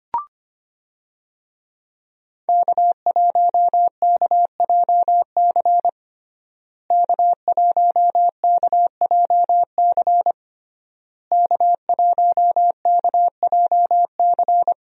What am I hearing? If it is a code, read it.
Morse code, heard as a keyed tone, about 25 wpm, 700 Hz. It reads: K1KJC K1KJC K1KJC
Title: USA Call Signs - 5 Characters - Repeated 3x - 25wpm